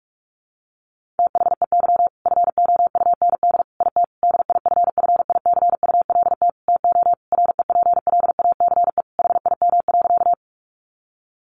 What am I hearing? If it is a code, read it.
Morse code, heard as a keyed tone, about 44 wpm, 700 Hz. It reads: THEY FOUND IT DIFFICULT TO REPLACE HIM.
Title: Sentences from Top 1400 Words 45wpm